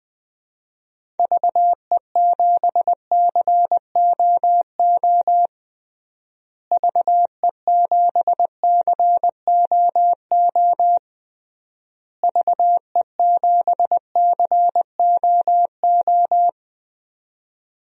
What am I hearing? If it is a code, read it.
Morse code, heard as a keyed tone, about 20 wpm, 700 Hz. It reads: VE7COO VE7COO VE7COO
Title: DXCC-100 Call Signs - 6 Characters - Repeated 3x - 20wpm